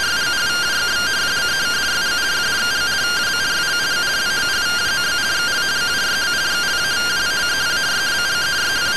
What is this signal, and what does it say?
Two-tone teleprinter signal, about 170 Hz shift.